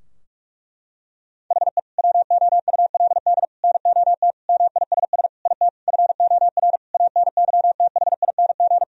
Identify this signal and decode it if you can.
Morse code, heard as a keyed tone, about 46 wpm, 700 Hz. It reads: HE WOULD NOT MISS IT FOR ANYTHING
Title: Sentences from Top 1000 Words - Review Part 3 - 45wpm